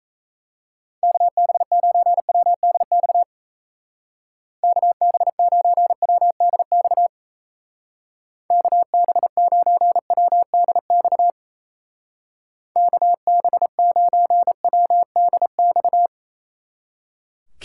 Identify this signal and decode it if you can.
Morse code, heard as a keyed tone, about 34 wpm, 700 Hz. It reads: KB9WDX KB9WDX KB9WDX KB9WDX
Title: Speed Racing - USA Call Signs 28wpm